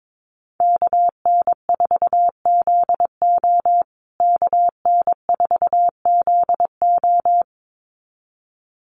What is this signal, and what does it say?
Morse code, heard as a keyed tone, about 22 wpm, 700 Hz. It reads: KN4ZO KN4ZO
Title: USA Call Signs Repeated 2x 22wpm